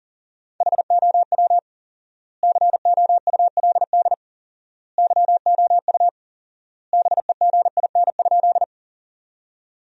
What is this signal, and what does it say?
Morse code, heard as a keyed tone, about 40 wpm, 700 Hz. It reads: HOW COULD YOU BEGIN?
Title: 4x Word-Spacing - Sentences from Top 300 Words - 40wpm